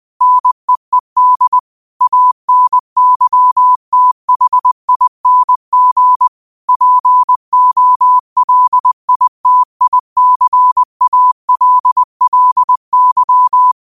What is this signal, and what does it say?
Morse code, heard as a keyed tone, about 20 wpm, 1000 Hz. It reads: NEED ANYTHING POLITICALLY